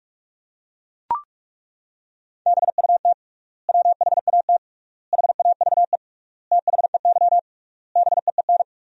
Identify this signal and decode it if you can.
Morse code, heard as a keyed tone, about 41 wpm, 690 Hz. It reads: BUT WHAT HAVE THEY BEEN
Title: 3x Word-Spacing - Sentences from Top 100 Words - 45wpm